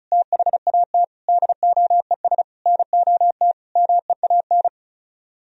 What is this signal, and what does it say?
Morse code, heard as a keyed tone, about 45 wpm, 700 Hz. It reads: THAT DOES NOT MEAN